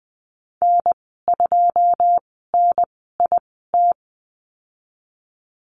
Morse code, heard as a keyed tone, about 20 wpm, spaced as for 14 wpm, 700 Hz. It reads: N2NIT
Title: 082 Lesson 14g - Number 2 - Call Signs 20wpm